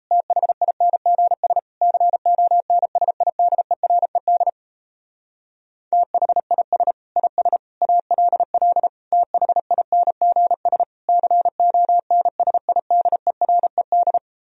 Morse code, heard as a keyed tone, about 38 wpm, 700 Hz. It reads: THINGS CONSIDERED THIS IS ALL THINGS CONSIDERED